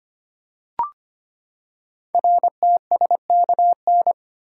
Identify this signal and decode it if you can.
Morse code, heard as a keyed tone, about 25 wpm, 700 Hz. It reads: RTSKN
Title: Code Groups - 5 Letters - 25wpm